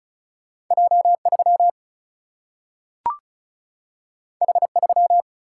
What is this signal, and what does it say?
Morse code, heard as a keyed tone, about 36 wpm, 700 Hz. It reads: J3 H3
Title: Single-Letter Single-Number 35wpm